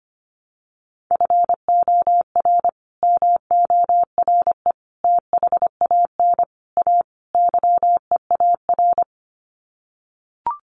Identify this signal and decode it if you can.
Morse code, heard as a keyed tone, about 25 wpm, 700 Hz. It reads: FOR MORE THAN A YEAR
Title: Sets of 5 Words 25wpm